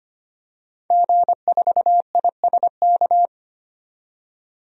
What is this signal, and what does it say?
Morse code, heard as a keyed tone, about 25 wpm, 700 Hz. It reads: G4ISK
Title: DXCC-100 Call Signs - 5 Characters - 25wpm